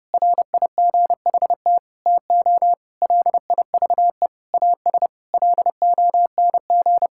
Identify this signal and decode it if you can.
Morse code, heard as a keyed tone, about 32 wpm, 700 Hz. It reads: RIGHT TO LIVE AS LONG
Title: Sentences from Top 1000 Words - Review Part 5 - 30wpm